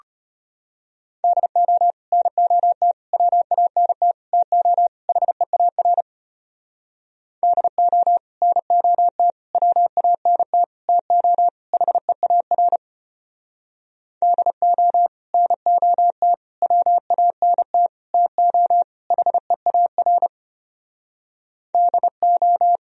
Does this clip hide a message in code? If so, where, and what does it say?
Yes: Morse code from 1.2 to 22.8 s, heard as a keyed tone, about 33 wpm, 700 Hz. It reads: DO NOT WANT TO HEAR DO NOT WANT TO HEAR DO NOT WANT TO HEAR DO